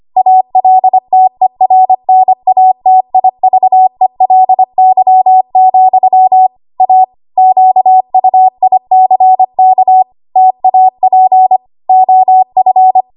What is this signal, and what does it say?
Morse code, heard as a keyed tone, about 25 wpm, 750 Hz. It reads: ALTERNATIVELY, A QUICK TAP OF
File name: Archive / 25 WPM / 25WPM